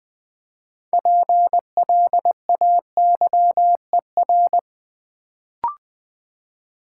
Morse code, heard as a keyed tone, about 20 wpm, 700 Hz. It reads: PLAYER
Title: Build A Word - 3 Letters At A Time - 6-Letter Words - 20wpm